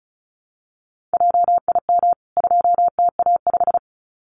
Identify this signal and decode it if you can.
Morse code, heard as a keyed tone, about 34 wpm, 700 Hz. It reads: JIM 2TA5
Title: CWT Exchange 35wpm